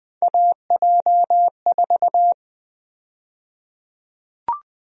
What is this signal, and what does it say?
Morse code, heard as a keyed tone, about 20 wpm, 700 Hz. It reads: AJ4
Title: USA Call Sign Prefixes - Repeated 2x - 20wpm